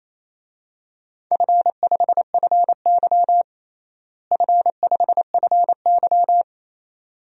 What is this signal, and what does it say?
Morse code, heard as a keyed tone, about 28 wpm, 700 Hz. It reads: F5FY F5FY
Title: DXCC-100 Call Signs - 4 Characters - Repeated 2x - 28wpm